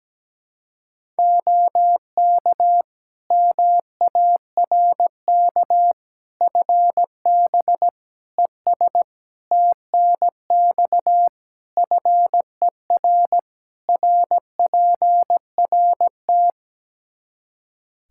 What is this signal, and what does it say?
Morse code, heard as a keyed tone, about 17 wpm, 700 Hz. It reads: OK MARK FB ES TNX FER RPRT
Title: Practice QSO - Exchange 2 - Info - 17wpm